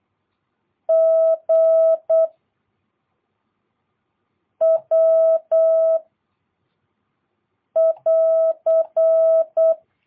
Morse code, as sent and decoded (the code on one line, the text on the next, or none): --. / .-- / .-.-.
G W <AR>